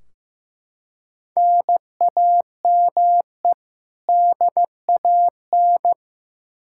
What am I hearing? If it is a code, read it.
Morse code, heard as a keyed tone, about 15 wpm, 700 Hz. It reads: NAME DAN
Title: Practice QSO - Exchange 1 - Beginning - 15wpm